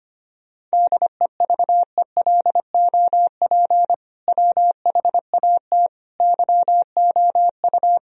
Morse code, heard as a keyed tone, about 25 wpm, 700 Hz. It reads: DEVELOP WHAT YOU
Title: Sentences from Top 1000 Words - Review Part 2 - 25wpm